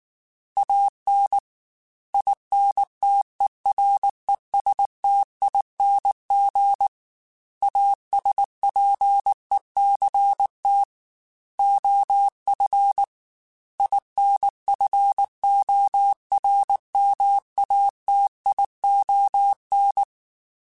Morse code, heard as a keyed tone, about 19 wpm, 800 Hz. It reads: AN INTERESTING ASPECT OF INFORMATION
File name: Technical Subjects / Technical Subjects-mp3s / Sw Pwr Sup @18